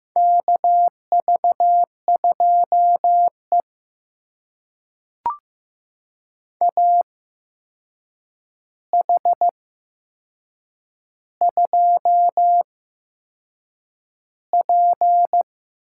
Morse code, heard as a keyed tone, about 15 wpm, spaced as for 6 wpm, 700 Hz. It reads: KV2E AH2P
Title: USA Call Signs - 4 Characters - 8x Character Spacing - 15wpm